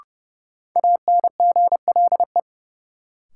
Morse code, heard as a keyed tone, about 32 wpm, 700 Hz. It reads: ANGLE
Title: Sentences from Top 1500 Words - New Word Drill Alt - 30wpm